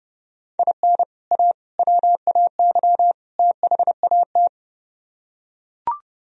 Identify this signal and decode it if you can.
Morse code, heard as a keyed tone, about 30 wpm, 700 Hz. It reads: IN A WAY THAT